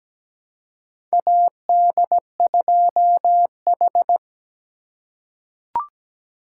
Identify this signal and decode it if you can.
Morse code, heard as a keyed tone, about 17 wpm, 700 Hz. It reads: AD2H
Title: USA Call Signs - 4 Characters - 3x Character Spacing - 17wpm